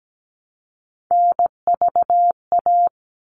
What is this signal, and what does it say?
Morse code, heard as a keyed tone, about 17 wpm, 700 Hz. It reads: NVA
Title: All Cut Numbers - 3-Digit 17wpm